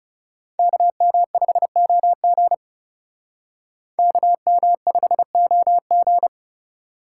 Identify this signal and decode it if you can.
Morse code, heard as a keyed tone, about 36 wpm, 700 Hz. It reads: KM5OG KM5OG
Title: Speed Racing - USA 2x2 Call Signs 30wpm